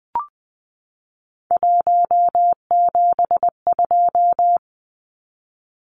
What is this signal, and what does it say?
Morse code, heard as a keyed tone, about 20 wpm, 700 Hz. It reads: 172
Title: Numbers - 3-Digit 20wpm